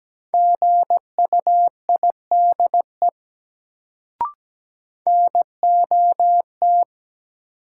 Morse code, heard as a keyed tone, about 17 wpm, 700 Hz. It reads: GUIDE NOT
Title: Sentences from Top 600 Words - New Word Drill - 17wpm